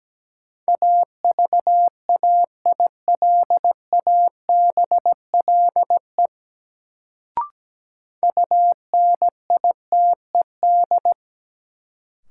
Morse code, heard as a keyed tone, about 17 wpm, 700 Hz. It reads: AVAILABLE UNITED
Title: Sentences from Top 600 Words - New Word Drill - 17wpm